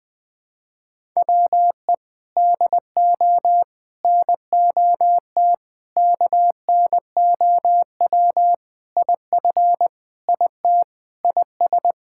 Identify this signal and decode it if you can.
Morse code, heard as a keyed tone, about 20 wpm, 700 Hz. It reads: WE DO NOT KNOW IF IT IS
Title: Sentences with 4-Word Phrases - 20wpm